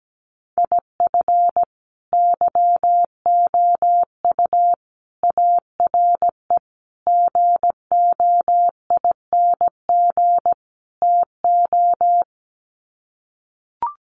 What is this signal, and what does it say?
Morse code, heard as a keyed tone, about 17 wpm, 700 Hz. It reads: IF YOU ARE GOING TO
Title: Sets of 5 Words Encore2 17wpm